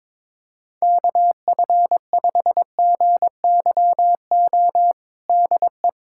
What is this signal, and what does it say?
Morse code, heard as a keyed tone, about 22 wpm, 700 Hz. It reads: KF5GYO DE
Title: USA Callsign DE Callsign 22wpm